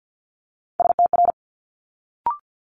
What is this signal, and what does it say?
Morse code, heard as a keyed tone, about 50 wpm, 700 Hz. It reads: STR
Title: Multi-letter Phonemes Quick Fire 50wpm